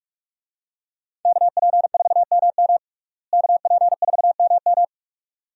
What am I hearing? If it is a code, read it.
Morse code, heard as a keyed tone, about 46 wpm, 700 Hz. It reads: KP4MM KP4MM
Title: USA Call Signs - 5 Characters - Repeated 2x - 45wpm